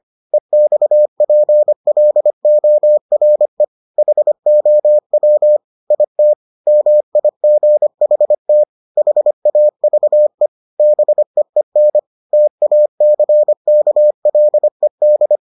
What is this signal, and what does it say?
Morse code, heard as a keyed tone, about 25 wpm, 600 Hz. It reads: EXPLORE HOW IT MIGHT HAVE BEEN TACKLED